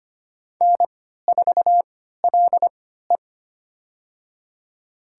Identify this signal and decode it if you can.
Morse code, heard as a keyed tone, about 25 wpm, spaced as for 14 wpm, 700 Hz. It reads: N4LE
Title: USA Call Signs - 4 Characters - 3x Character Spacing - 25wpm